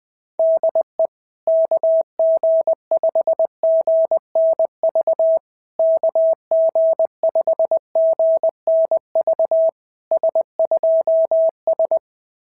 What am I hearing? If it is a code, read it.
Morse code, heard as a keyed tone, about 20 wpm, 660 Hz. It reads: DE KG5GNV KG5GNV S2S
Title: SOTA QSO Practice 20wpm